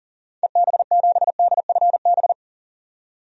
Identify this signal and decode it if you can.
Morse code, heard as a keyed tone, about 40 wpm, 700 Hz. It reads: EB7DFB
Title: DXCC-100 Call Signs - 6 Characters - 40wpm